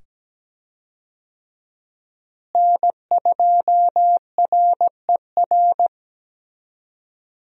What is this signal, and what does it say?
Morse code, heard as a keyed tone, about 17 wpm, 700 Hz. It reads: N2RER
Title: USA Call Signs - 5 Characters - Repeated 2x - 17wpm